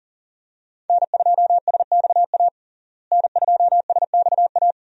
Morse code, heard as a keyed tone, about 40 wpm, 700 Hz. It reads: N2SXA N2SXA
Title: USA Call Signs - 5 Characters - Repeated 2x - 40wpm